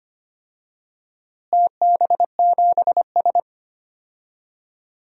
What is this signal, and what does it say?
Morse code, heard as a keyed tone, about 25 wpm, 700 Hz. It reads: TB7S
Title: DXCC-100 Call Signs - 4 Characters - Repeated 3x - 25wpm